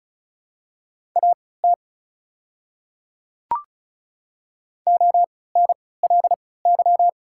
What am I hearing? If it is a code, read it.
Morse code, heard as a keyed tone, about 36 wpm, spaced as for 20 wpm, 700 Hz. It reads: AT ONLY
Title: 3x Character Spacing - Top 200 Words - 35wpm